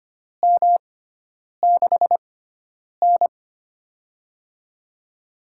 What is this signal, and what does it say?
Morse code, heard as a keyed tone, about 25 wpm, spaced as for 9 wpm, 700 Hz. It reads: M6N
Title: USA Call Signs - 4 Characters - 6x Character Spacing - 25wpm